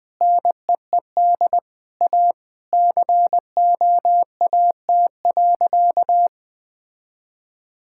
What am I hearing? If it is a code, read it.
Morse code, heard as a keyed tone, about 20 wpm, 700 Hz. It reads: NEED A COAT.
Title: Sentences with 4-Word Phrases - Spoken Beginning - 20wpm